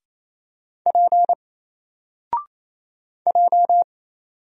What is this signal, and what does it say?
Morse code, heard as a keyed tone, about 27 wpm, 700 Hz. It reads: P J
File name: Single-Letter-Number-28wpm